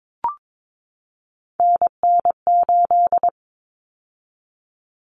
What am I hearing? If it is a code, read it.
Morse code, heard as a keyed tone, about 22 wpm, 700 Hz. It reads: NN8
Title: Common Cut Numbers - 3-Digit 22wpm